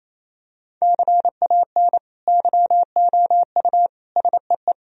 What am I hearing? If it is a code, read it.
Morse code, heard as a keyed tone, about 28 wpm, 700 Hz. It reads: CAN YOU SEE
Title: Sentences from Top 1000 Words - Review Part 1 - 28wpm